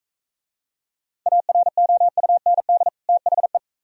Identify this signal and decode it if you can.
Morse code, heard as a keyed tone, about 44 wpm, 700 Hz. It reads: AROUND THE